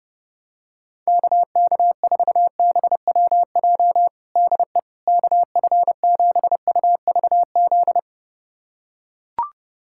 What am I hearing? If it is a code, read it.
Morse code, heard as a keyed tone, about 30 wpm, 700 Hz. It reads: KK4BWJ DE KF7UVZ
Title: USA Callsign DE Callsign - 6 Characters - 30wpm